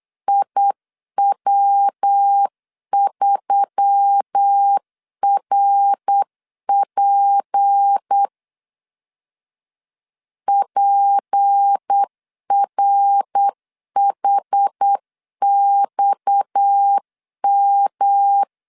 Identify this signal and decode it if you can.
Morse code, heard as a keyed tone, about 8 wpm, 800 Hz. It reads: IW3RP PRHXM
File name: CW / DET / 1 6 1500C S C 8wpm 5219